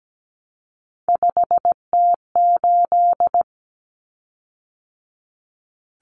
Morse code, heard as a keyed tone, about 17 wpm, 700 Hz. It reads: 5T8
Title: Common Cut Numbers - 3-Digit 17wpm